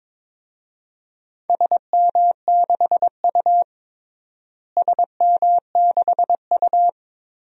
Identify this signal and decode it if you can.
Morse code, heard as a keyed tone, about 22 wpm, 700 Hz. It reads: SM6U SM6U
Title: DXCC-100 Call Signs - 4 Characters - Repeated 2x - 22wpm